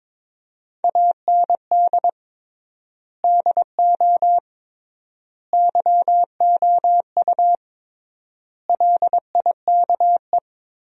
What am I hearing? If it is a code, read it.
Morse code, heard as a keyed tone, about 21 wpm, 700 Hz. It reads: AND DO YOU LIKE
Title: 3x Word-Spacing - Sentences from Top 100 Words - 22wpm